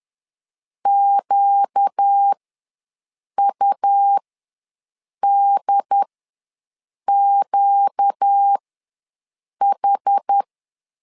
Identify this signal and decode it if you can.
Morse code, heard as a keyed tone, about 10 wpm, spaced as for 6 wpm, 800 Hz. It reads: QUDQH
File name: LEZIONI / 4 Q ws 10wpm 2306